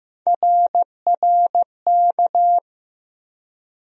Morse code, heard as a keyed tone, about 15 wpm, 700 Hz. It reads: RRK
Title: USA Call Sign Postfixes - 3 Letters 15wpm